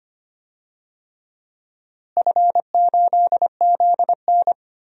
Morse code, heard as a keyed tone, about 25 wpm, 700 Hz. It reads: F8ZN
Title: DXCC-100 Call Signs - 4 Characters - Repeated 3x - 25wpm